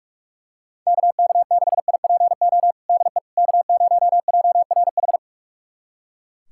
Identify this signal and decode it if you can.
Morse code, heard as a keyed tone, about 43 wpm, 700 Hz. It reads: KK6IPO DE K0JRH
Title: USA Callsign DE Callsign 45wpm